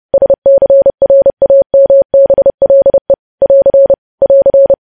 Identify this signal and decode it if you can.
Morse code, heard as a keyed tone, about 30 wpm, 550 Hz. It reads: SCRAMBLE <AR> <AR>
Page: news headlines in morse code at 30 wpm